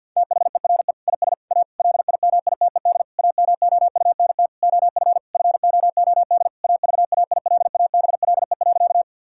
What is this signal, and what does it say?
Morse code, heard as a keyed tone, about 50 wpm, 700 Hz. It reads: THERE IS A LIMITED AMOUNT OF FOOD AVAILABLE.
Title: Sentences from Top 1000 Words - Review Part 6 - 50wpm